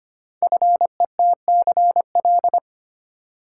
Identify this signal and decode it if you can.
Morse code, heard as a keyed tone, about 25 wpm, 700 Hz. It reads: FETCL
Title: Code Groups - 5 Letters - 25wpm